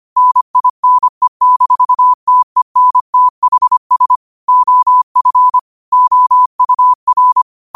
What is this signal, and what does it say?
Morse code, heard as a keyed tone, about 25 wpm, 1000 Hz. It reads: NINE-TENTHS OF OUR
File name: qotdfiles / 25wpm